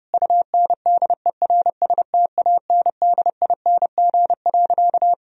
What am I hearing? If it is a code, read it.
Morse code, heard as a keyed tone, about 32 wpm, 700 Hz. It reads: UNDERSTANDING.
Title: Sentences from Top 1000 Words - Review Part 3 - 30wpm